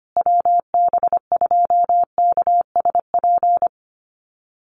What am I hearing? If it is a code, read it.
Morse code, heard as a keyed tone, about 25 wpm, 700 Hz. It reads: WB2KSP
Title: USA Call Signs Repeated 3x 25wpm